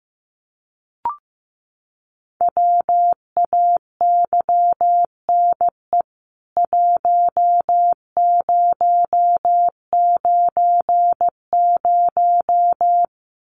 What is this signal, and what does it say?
Morse code, heard as a keyed tone, about 15 wpm, 700 Hz. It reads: WAYNE 1090
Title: CWT Exchange 15wpm